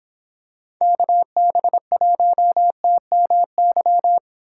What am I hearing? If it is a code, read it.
Morse code, heard as a keyed tone, about 25 wpm, 700 Hz. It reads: KB1TMY